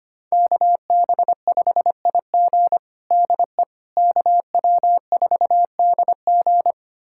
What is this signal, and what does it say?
Morse code, heard as a keyed tone, about 25 wpm, 700 Hz. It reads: KB5IG DE KW4DG
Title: USA Callsign DE Callsign - 5 Characters - 25wpm